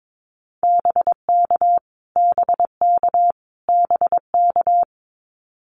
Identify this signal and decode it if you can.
Morse code, heard as a keyed tone, about 22 wpm, 700 Hz. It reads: BK BK BK
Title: Abbreviations Repeated 3x - 22wpm